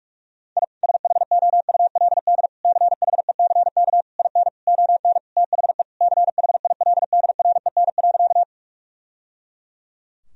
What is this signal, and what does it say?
Morse code, heard as a keyed tone, about 46 wpm, 700 Hz. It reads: I SHOULD CHECK IN ON THE CHILDREN.
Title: Sentences from Top 1000 Words - Review Part 3 - 45wpm